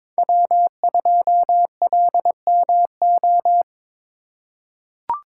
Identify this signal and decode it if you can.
Morse code, heard as a keyed tone, about 22 wpm, 700 Hz. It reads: W2LMO
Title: USA Call Signs - 5 Characters - 2x Character Spacing - 22wpm